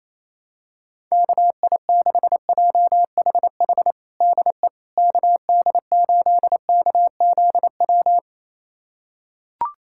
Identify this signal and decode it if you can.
Morse code, heard as a keyed tone, about 28 wpm, 700 Hz. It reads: KI6JHH DE KD8KZW
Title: USA Callsign DE Callsign 28wpm